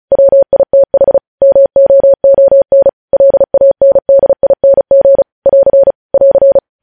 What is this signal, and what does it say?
Morse code, heard as a keyed tone, about 36 wpm, 550 Hz. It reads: WITH MOON LANDING <AR> <AR>
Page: News Headlines in Morse Code at 35 WPM